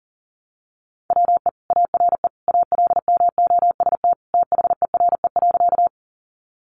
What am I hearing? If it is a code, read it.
Morse code, heard as a keyed tone, about 40 wpm, 700 Hz. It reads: WE ARE ALMOST THERE.
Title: Sentences from Top 700 Words 40wpm